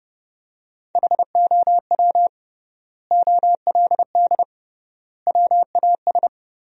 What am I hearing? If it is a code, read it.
Morse code, heard as a keyed tone, about 30 wpm, 700 Hz. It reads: HOW OLD WAS
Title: 3x Word-Spacing - Sentences from Top 200 Words - 30wpm